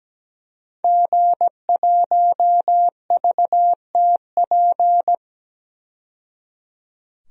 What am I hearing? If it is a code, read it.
Morse code, heard as a keyed tone, about 17 wpm, 700 Hz. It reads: G1VTP
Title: DXCC-100 Call Signs - 5 Characters - 17wpm